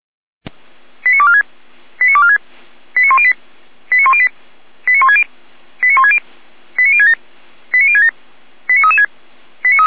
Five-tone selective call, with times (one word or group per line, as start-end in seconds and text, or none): 1.0-1.4 89246
2.0-2.4 89246
3.0-3.3 89108
3.9-4.3 89108
4.9-5.2 89160
5.8-6.2 89160
6.8-7.1 89067
7.7-8.1 89067
8.7-9.1 89336